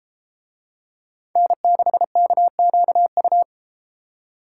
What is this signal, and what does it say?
Morse code, heard as a keyed tone, about 32 wpm, 700 Hz. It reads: N6KQU